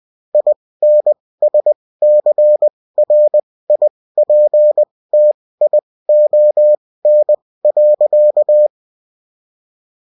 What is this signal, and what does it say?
Morse code, heard as a keyed tone, about 20 wpm, spaced as for 16 wpm, 600 Hz. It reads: INSCRIPTION.